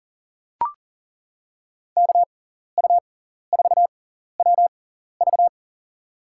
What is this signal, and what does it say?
Morse code, heard as a keyed tone, about 39 wpm, spaced as for 14 wpm, 700 Hz. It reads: KU4WV
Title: USA Call Signs - 5 Characters - 6x Character Spacing - 40wpm